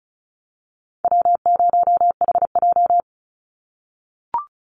Morse code, heard as a keyed tone, about 34 wpm, 700 Hz. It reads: W0HJ
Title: USA Call Signs 35wpm